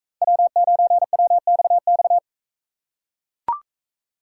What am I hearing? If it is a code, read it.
Morse code, heard as a keyed tone, about 39 wpm, 700 Hz. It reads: W9WXX